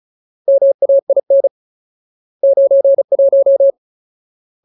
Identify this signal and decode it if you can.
Morse code, heard as a keyed tone, about 34 wpm, 550 Hz. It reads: MAIN 91
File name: Ultimate Word List / 35WPM Ultimate